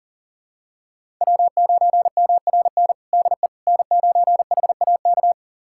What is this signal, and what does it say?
Morse code, heard as a keyed tone, about 40 wpm, 700 Hz. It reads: W9MRN DE N9HAK
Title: USA Callsign DE Callsign - 5 Characters - 40wpm